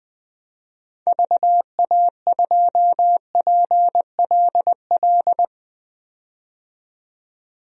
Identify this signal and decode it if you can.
Morse code, heard as a keyed tone, about 20 wpm, 700 Hz. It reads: VA2PLL
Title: DXCC-100 Call Signs - 6 Characters - 20wpm